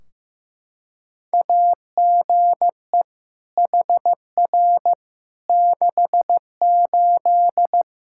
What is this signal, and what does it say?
Morse code, heard as a keyed tone, about 16 wpm, 700 Hz. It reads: AGE HR 68
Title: Practice QSO - Exchange 3 - More Info - 15wpm